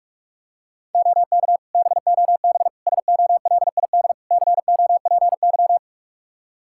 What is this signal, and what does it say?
Morse code, heard as a keyed tone, about 44 wpm, 700 Hz. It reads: OK BOB SOLID COPY
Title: Practice QSO - Exchange 3 - More Info - 45wpm